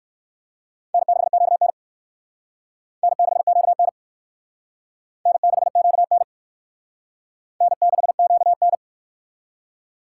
Morse code, heard as a keyed tone, about 54 wpm, 700 Hz. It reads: N6QN N6QN N6QN N6QN
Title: Speed Racing - USA 1x2 Call Signs 45wpm